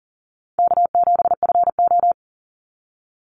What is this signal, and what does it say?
Morse code, heard as a keyed tone, about 40 wpm, 700 Hz. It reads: K7FO
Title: NA SOTA Chaser Callsigns 40wpm